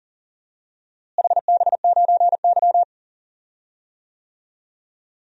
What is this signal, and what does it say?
Morse code, heard as a keyed tone, about 40 wpm, 700 Hz. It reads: HB9Y
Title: DXCC-100 Call Signs - 4 Characters - Repeated 3x - 40wpm